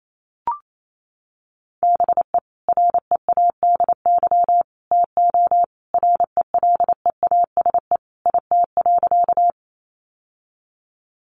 Morse code, heard as a keyed tone, about 28 wpm, 700 Hz. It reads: BE READY TO RELEASE IT.
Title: Sentences from Top 1100 Words 28wpm